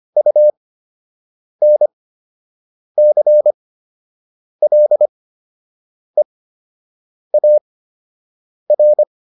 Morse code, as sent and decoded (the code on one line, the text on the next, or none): ..- -. -.-. .-.. .- .-.
UNCLEAR